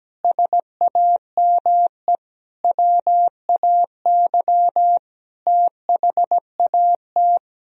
Morse code, heard as a keyed tone, about 17 wpm, 700 Hz. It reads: SAME WAY THAT
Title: Speed-Racing - Sets of 5 Words 17wpm